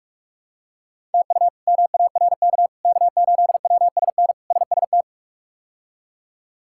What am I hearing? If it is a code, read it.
Morse code, heard as a keyed tone, about 42 wpm, 700 Hz. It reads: TU MARK K8WSN SST
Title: K1USN SST QSO Practice - 45wpm